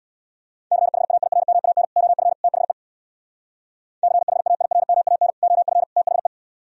Morse code, heard as a keyed tone, about 64 wpm, 700 Hz. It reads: CHAIRMAN OF THE CHAIRMAN OF THE